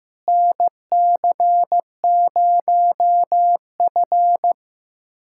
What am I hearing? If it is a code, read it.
Morse code, heard as a keyed tone, about 16 wpm, 700 Hz. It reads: NC0F
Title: USA Callsign DE Callsign - 4 Characters - 15wpm